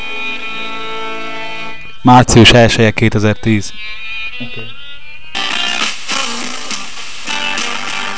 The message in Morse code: none